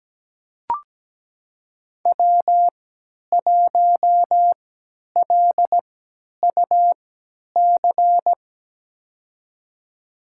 Morse code, heard as a keyed tone, about 17 wpm, spaced as for 10 wpm, 700 Hz. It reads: W1LUC
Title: USA Call Signs - 5 Characters - 3x Character Spacing - 17wpm